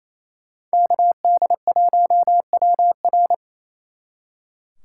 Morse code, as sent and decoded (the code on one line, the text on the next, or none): -.- -.. .---- .-- .-.
KD1WR